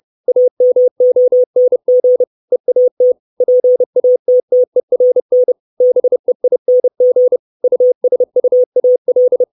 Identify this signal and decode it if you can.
Morse code, heard as a keyed tone, about 30 wpm, 500 Hz. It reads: AMONG EAT PATTERN BEING USUAL
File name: 632 Words 30WPM